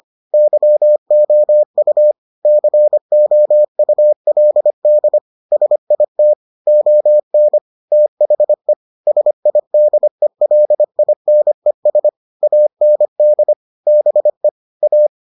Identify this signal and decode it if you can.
Morse code, heard as a keyed tone, about 25 wpm, 600 Hz. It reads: YOU COULD SIT ON THE SIDELINES AND BE A